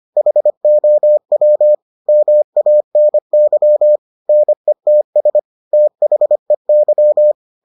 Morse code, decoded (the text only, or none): HOW MANY NETS THEY